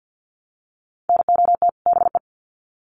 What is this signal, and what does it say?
Morse code, heard as a keyed tone, about 50 wpm, 700 Hz. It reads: NOT BE
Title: Sets of 2 Words 50wpm